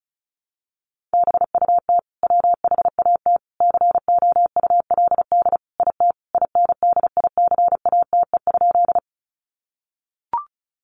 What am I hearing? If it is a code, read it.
Morse code, heard as a keyed tone, about 34 wpm, 700 Hz. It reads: BUT WHAT COULD IT INDICATE?
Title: Sentences from Top 1300 Words 35wpm